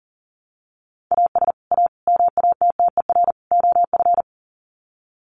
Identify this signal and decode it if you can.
Morse code, heard as a keyed tone, about 40 wpm, 700 Hz. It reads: AS A MATTER OF
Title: Sets of 4 Words 40wpm